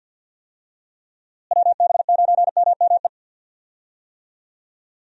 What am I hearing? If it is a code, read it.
Morse code, heard as a keyed tone, about 52 wpm, 700 Hz. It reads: WB9MME
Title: USA Call Signs - 6 Characters - Repeated 3x - 50wpm